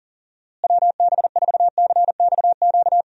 Morse code, heard as a keyed tone, about 40 wpm, 700 Hz. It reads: WB4CXQ
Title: USA Call Signs - 6 Characters - Repeated 2x - 40wpm